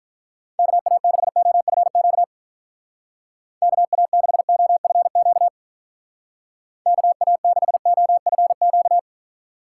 Morse code, heard as a keyed tone, about 45 wpm, 700 Hz. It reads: KA6OFQ KA6OFQ KA6OFQ